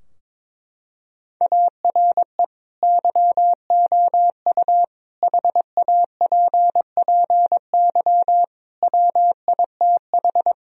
Morse code, heard as a keyed tone, about 22 wpm, 700 Hz. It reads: ARE YOU HAPPY WITH
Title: Sentences from Top 1000 Words - Review Part 2 - 22wpm